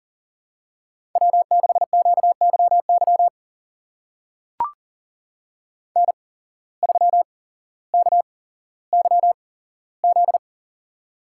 Morse code, heard as a keyed tone, about 41 wpm, spaced as for 17 wpm, 700 Hz. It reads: W6QYY N3KYZ
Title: USA Call Signs - 5 Characters - 8x Character Spacing - 40wpm